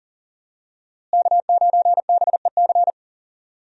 Morse code, heard as a keyed tone, about 40 wpm, 700 Hz. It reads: K9BEC
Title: USA Call Signs - 5 Characters - Repeated 3x - 40wpm